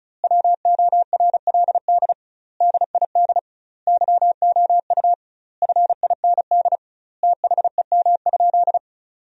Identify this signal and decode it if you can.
Morse code, heard as a keyed tone, about 34 wpm, 700 Hz. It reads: WORLD DID YOU FIND THEM?
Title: 2x Word-Spacing - Sentences from Top 200 Words - 35wpm